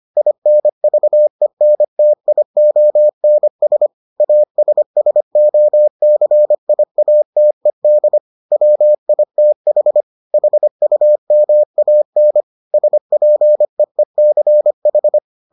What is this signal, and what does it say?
Morse code, heard as a keyed tone, about 25 wpm, 600 Hz. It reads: INVENTIONS ASSOCIATED WITH HUMAN SPEECH